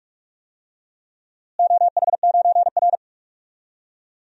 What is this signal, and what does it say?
Morse code, heard as a keyed tone, about 43 wpm, 700 Hz. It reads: OH9R
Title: DXCC-100 Call Signs - 4 Characters - Repeated 3x - 45wpm